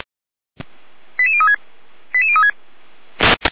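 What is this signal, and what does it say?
Five-tone selective call (ZVEI), heard as quick runs of five tones, tones of about 70 ms. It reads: 88036 88036
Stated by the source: ZVEI five-tone selective call received with an SDR.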